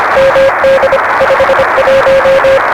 Morse code, heard as a keyed tone, about 25 wpm, 560 Hz, with noise. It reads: MD 51
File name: RS7 Beacon 16 Feb 1985